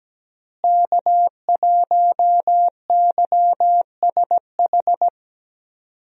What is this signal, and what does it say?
Morse code, heard as a keyed tone, about 17 wpm, 700 Hz. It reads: K1YSH
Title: USA Call Signs - 5 Characters - Repeated 2x - 17wpm